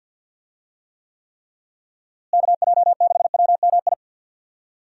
Morse code, heard as a keyed tone, about 50 wpm, 700 Hz. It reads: KJ6WMI
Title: USA Call Signs - 6 Characters - Repeated 3x - 50wpm